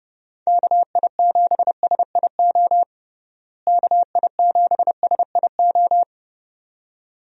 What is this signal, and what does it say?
Morse code, heard as a keyed tone, about 30 wpm, 700 Hz. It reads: KI7SIO KI7SIO
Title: USA Call Signs - 6 Characters - Repeated 2x - 30wpm